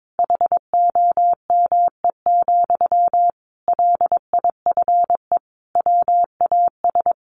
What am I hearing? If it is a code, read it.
Morse code, heard as a keyed tone, about 22 wpm, 700 Hz. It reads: HOME, LIFE WAS
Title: Sentences from Top 1400 Words 22wpm